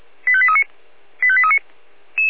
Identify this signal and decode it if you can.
Five-tone selective call (ZVEI), heard as quick runs of five tones, tones of about 70 ms. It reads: 85839 85839